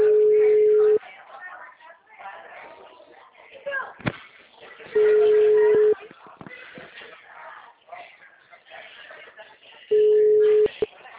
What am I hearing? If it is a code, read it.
Ringback tone (Europe, 425 Hz), with speech.
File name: out- -2006- - - .388